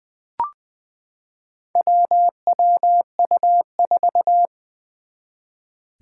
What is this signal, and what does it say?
Morse code, heard as a keyed tone, about 20 wpm, 700 Hz. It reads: WWU4
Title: Four Challenging Letter-Number Combinations 20wpm